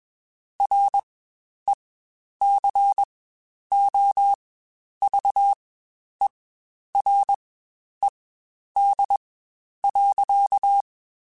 Morse code, heard as a keyed tone, about 21 wpm, spaced as for 10 wpm, 800 Hz. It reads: RECOVERED.
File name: Quotes Spaced / Quotes Spaced-mp3s / Character sp @20